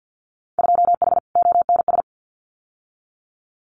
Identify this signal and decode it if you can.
Morse code, heard as a keyed tone, about 50 wpm, 700 Hz. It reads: 2H ONS
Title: Winter Field Day Exchanges Encore 50wpm